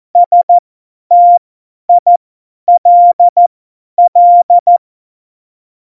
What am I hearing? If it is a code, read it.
Morse code, heard as a keyed tone, about 14 wpm, spaced as for 10 wpm, 700 Hz. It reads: STILL
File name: Morsekurs / Witze / Witz 16. April 2019, 50 F70